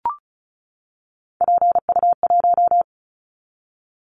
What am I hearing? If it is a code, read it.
Morse code, heard as a keyed tone, about 36 wpm, 690 Hz. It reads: PU1
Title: Callsign Prefixes for Top 50 DXCC - 35wpm